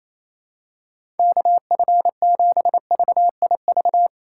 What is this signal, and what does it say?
Morse code, heard as a keyed tone, about 28 wpm, 700 Hz. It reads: KF7VIV
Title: USA Call Signs - 6 Characters - 2x Character Spacing - 28wpm